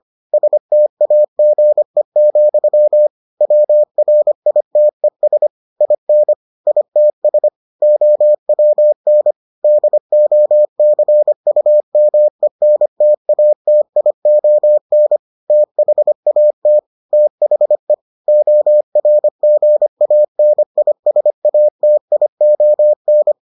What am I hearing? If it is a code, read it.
Morse code, heard as a keyed tone, about 25 wpm, 600 Hz. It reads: STAGE, WRITES IN ITS OWN DOCUMENTATION THAT THE ORGANISATION